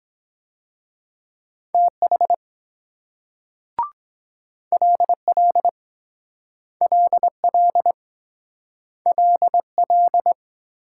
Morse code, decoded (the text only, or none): TH LL LL LL